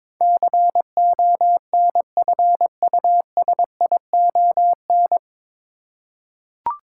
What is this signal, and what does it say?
Morse code, heard as a keyed tone, about 22 wpm, 700 Hz. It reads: CONFUSION
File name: 9 Letter Words - First 3 Letters Given - 22wpm